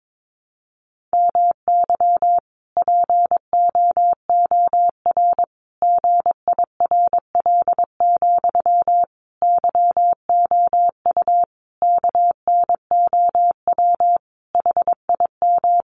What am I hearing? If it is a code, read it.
Morse code, heard as a keyed tone, about 22 wpm, 700 Hz. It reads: MY POOR GIRL, YOU KNOW HIM
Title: Sentences from Top 1000 Words 22wpm